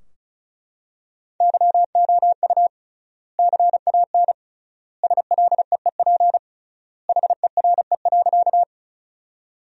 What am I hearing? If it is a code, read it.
Morse code, heard as a keyed tone, about 36 wpm, 700 Hz. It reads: YOU CAN SLEEP HERE.
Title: 3x Word-Spacing - Sentences from Top 500 Words - 35wpm